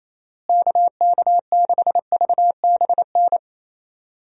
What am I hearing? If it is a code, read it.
Morse code, heard as a keyed tone, about 27 wpm, 700 Hz. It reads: KK6VBN
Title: USA Call Signs - 6 Characters - Repeated 3x - 28wpm